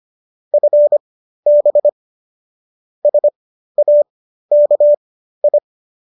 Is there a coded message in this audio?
Yes: Morse code from 0.5 to 5.6 s, heard as a keyed tone, about 25 wpm, spaced as for 13 wpm, 590 Hz. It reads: FB SAKI